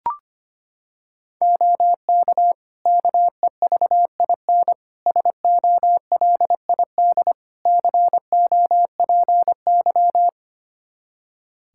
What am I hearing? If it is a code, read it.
Morse code, heard as a keyed tone, about 25 wpm, 700 Hz. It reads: OK KEVIN SOLID COPY
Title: Practice QSO - Exchange 3 - More Info - 25wpm